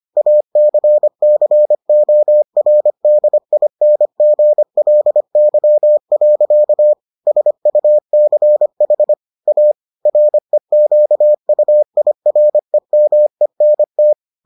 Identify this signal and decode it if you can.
Morse code, heard as a keyed tone, about 25 wpm, 600 Hz. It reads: ACCORDINGLY. SUCH A REQUIREMENT